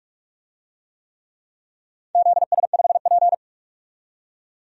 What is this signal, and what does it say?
Morse code, heard as a keyed tone, about 43 wpm, 700 Hz. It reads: ZS5P
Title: DXCC-100 Call Signs - 4 Characters - 45wpm